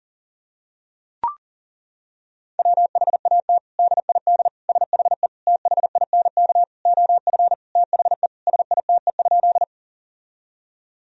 Morse code, heard as a keyed tone, about 40 wpm, 700 Hz. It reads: WHAT DID SHE THINK OF THE SITE?
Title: Sentences from Top 1000 Words - Review Part 4 - 40wpm